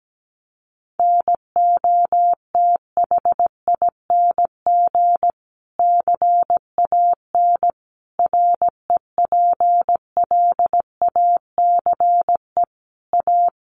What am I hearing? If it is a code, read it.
Morse code, heard as a keyed tone, about 17 wpm, 700 Hz. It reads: NOTHING CAN REPLACE A
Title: Sentences from Top 1400 Words 17wpm